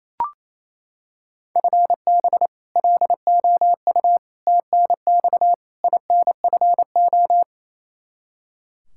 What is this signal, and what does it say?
Morse code, heard as a keyed tone, about 28 wpm, 700 Hz. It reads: FB LOU TNX INFO
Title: K1USN SST QSO Practice 28wpm